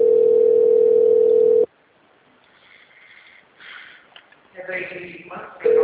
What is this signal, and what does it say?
Ringback tone (North America), 440 + 480 Hz, with speech.
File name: out- -2017- - - .3432